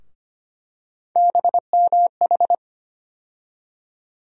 Morse code, heard as a keyed tone, about 25 wpm, 700 Hz. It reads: BMH
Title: USA Call Sign Postfixes - 3 Letters 25wpm